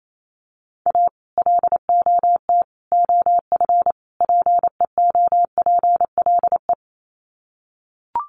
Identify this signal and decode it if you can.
Morse code, heard as a keyed tone, about 28 wpm, 700 Hz. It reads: A LOT OF PEOPLE
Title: Sets of 4 Words 28wpm